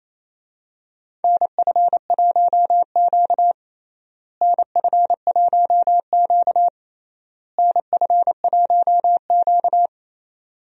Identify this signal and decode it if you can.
Morse code, heard as a keyed tone, about 27 wpm, 700 Hz. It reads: NF1Q NF1Q NF1Q
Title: USA Call Signs - 4 Characters - Repeated 3x - 28wpm